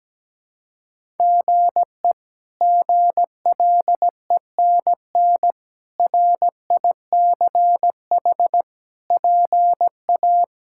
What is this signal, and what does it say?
Morse code, heard as a keyed tone, about 17 wpm, 700 Hz. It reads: GE GLENN RICH PA
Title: K1USN SST QSO Practice - 17wpm